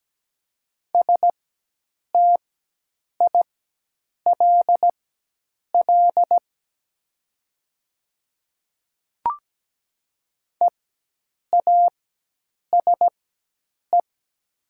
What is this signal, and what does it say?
Morse code, heard as a keyed tone, about 17 wpm, spaced as for 8 wpm, 700 Hz. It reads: STILL EASE
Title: 4x Character Spacing - Top 300 Words - 17wpm